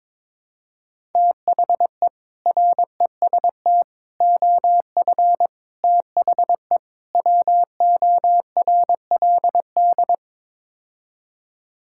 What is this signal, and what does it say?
Morse code, heard as a keyed tone, about 22 wpm, 700 Hz. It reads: THE REST OF THE WORLD
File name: practice / 22wpm / Build A Phrase - 5-Word Phrases - 22wpm